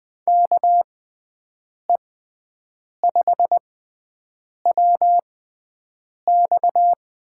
Morse code, heard as a keyed tone, about 20 wpm, spaced as for 7 wpm, 700 Hz. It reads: KE5WX
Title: USA Call Signs - 6 Characters - 6x Character Spacing - 20wpm